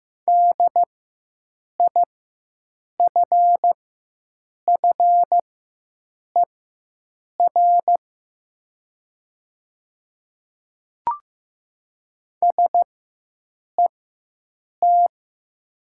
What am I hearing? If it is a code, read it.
Morse code, heard as a keyed tone, about 15 wpm, spaced as for 7 wpm, 700 Hz. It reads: DIFFER SET
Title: 4x Character Spacing - Top 300 Words - 15wpm